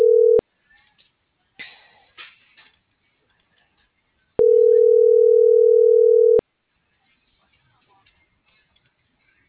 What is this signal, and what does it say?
Ringback tone (North America), 440 + 480 Hz.